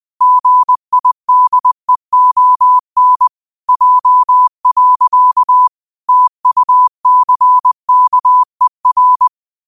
Morse code, heard as a keyed tone, about 20 wpm, 1000 Hz. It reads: GIDEON J. TUCKER
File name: qotdfiles / 20wpm